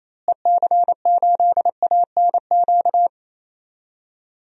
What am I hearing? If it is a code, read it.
Morse code, heard as a keyed tone, about 27 wpm, 700 Hz. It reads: EC8ANQ
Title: DXCC-100 Call Signs - 6 Characters - Repeated 2x - 28wpm